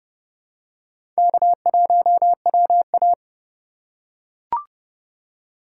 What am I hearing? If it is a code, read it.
Morse code, heard as a keyed tone, about 32 wpm, 700 Hz. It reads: K1WA
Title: USA Call Signs - 2x Character Spacing - 30wpm